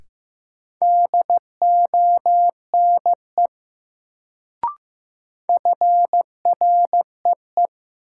Morse code, heard as a keyed tone, about 15 wpm, 700 Hz. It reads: DONE FREE
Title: Sentences from Top 500 Words - New Word Drill - 15wpm